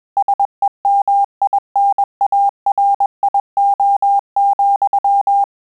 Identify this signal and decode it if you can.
Morse code, heard as a keyed tone, about 21 wpm, 800 Hz. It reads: SEMINARIO,
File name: Telegraphy-group / breve testo midi